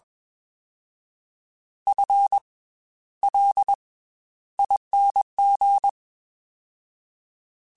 Morse code, heard as a keyed tone, about 21 wpm, 800 Hz. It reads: F L ING